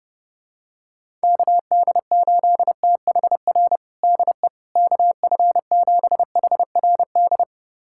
Morse code, heard as a keyed tone, about 30 wpm, 700 Hz. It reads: KD8THR DE KF7HRD
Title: USA Callsign DE Callsign - 6 Characters - 30wpm